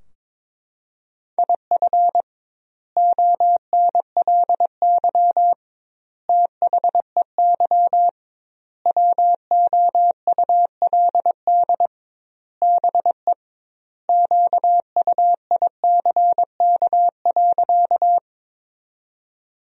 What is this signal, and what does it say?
Morse code, heard as a keyed tone, about 22 wpm, 700 Hz. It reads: IF ONLY THEY WOULD BE QUICK.
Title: 2x Word-Spacing - Sentences from Top 500 Words - 22wpm